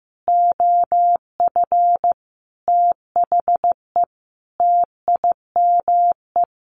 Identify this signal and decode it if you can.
Morse code, heard as a keyed tone, about 15 wpm, 700 Hz. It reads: OF THE TIME
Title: A Sound Of Thunder 15wpm